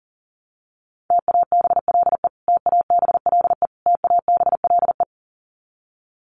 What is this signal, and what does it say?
Morse code, heard as a keyed tone, about 40 wpm, 700 Hz. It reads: TABLE TABLE TABLE
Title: Top 500 Words Repeated 40wpm